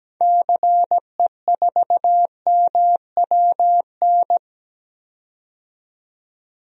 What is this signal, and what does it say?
Morse code, heard as a keyed tone, about 17 wpm, 700 Hz. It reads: CE4MWN
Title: DXCC-100 Call Signs - 6 Characters - 17wpm